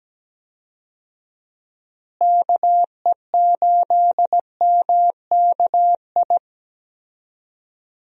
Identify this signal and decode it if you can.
Morse code, heard as a keyed tone, about 17 wpm, 700 Hz. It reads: KE8MKI
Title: USA Call Signs - 6 Characters - Repeated 3x - 17wpm